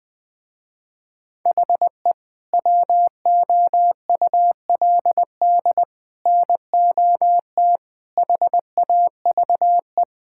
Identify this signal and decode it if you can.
Morse code, heard as a keyed tone, about 20 wpm, 700 Hz. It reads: HE WOULD NOT HAVE
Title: Sentences with 3-Word Phrases - Spoken Beginning - 20wpm